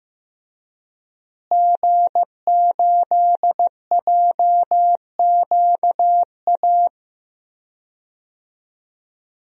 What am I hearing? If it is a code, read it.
Morse code, heard as a keyed tone, about 15 wpm, 700 Hz. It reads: G8JQA
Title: DXCC-100 Call Signs - 5 Characters - Repeated 3x - 15wpm